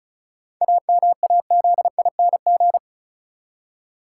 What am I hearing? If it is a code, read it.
Morse code, heard as a keyed tone, about 36 wpm, 700 Hz. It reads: AMAZING
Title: Seven Letter Words - 35wpm